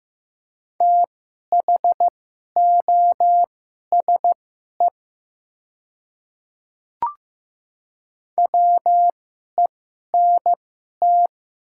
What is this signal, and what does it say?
Morse code, heard as a keyed tone, about 15 wpm, spaced as for 11 wpm, 700 Hz. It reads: THOSE WENT
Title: 2x Character Spacing - Top 500 Words - 15wpm